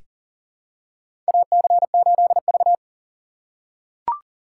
Morse code, heard as a keyed tone, about 40 wpm, 700 Hz. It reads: AC8V
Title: USA Call Signs - 4 Characters - 4x Character Spacing - 40wpm